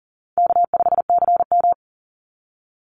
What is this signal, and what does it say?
Morse code, heard as a keyed tone, about 40 wpm, 700 Hz. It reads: K5CM
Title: Call Signs Repeated 3x - 40wpm